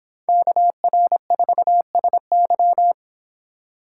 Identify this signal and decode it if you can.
Morse code, heard as a keyed tone, about 32 wpm, 690 Hz. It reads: KR4SY